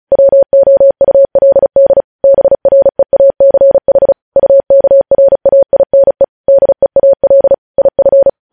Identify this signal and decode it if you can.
Morse code, heard as a keyed tone, about 36 wpm, 550 Hz. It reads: WOULD BREACH UKRAINE DEAL IF
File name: podcast / News headlines-1x-35x35